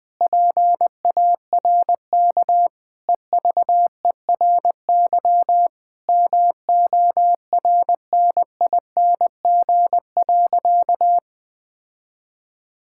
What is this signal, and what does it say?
Morse code, heard as a keyed tone, about 19 wpm, 700 Hz. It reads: PARK EVERY MORNING.
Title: Sentences with 2-Word Phrases - Spoken Beginning - 20wpm